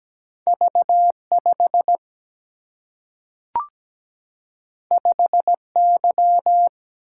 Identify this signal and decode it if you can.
Morse code, heard as a keyed tone, about 17 wpm, 700 Hz. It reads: V 5 5 Y
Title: Single-Letter Single-Number 17wpm